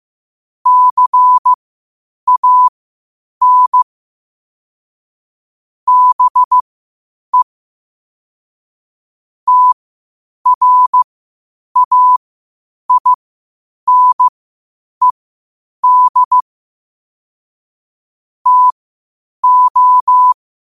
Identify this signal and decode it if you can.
Morse code, heard as a keyed tone, about 15 wpm, spaced as for 8 wpm, 1000 Hz. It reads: CAN BE TRAINED TO
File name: qotdfiles / 8wpm